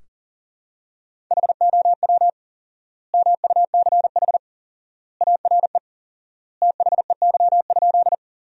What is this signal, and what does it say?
Morse code, heard as a keyed tone, about 40 wpm, 700 Hz. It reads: HOW MUCH ARE THEY?
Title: 4x Word-Spacing - Sentences from Top 200 Words - 40wpm